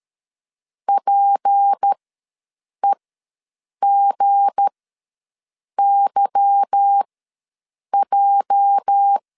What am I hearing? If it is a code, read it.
Morse code, heard as a keyed tone, about 13 wpm, spaced as for 7 wpm, 800 Hz. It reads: PEGYJ
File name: LEZIONI / 8 YJ45 / 8 YJ ws 12wpm 3342